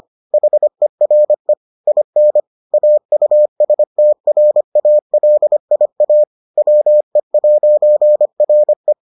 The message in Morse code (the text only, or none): HERE IN AUSTRALIA WE'RE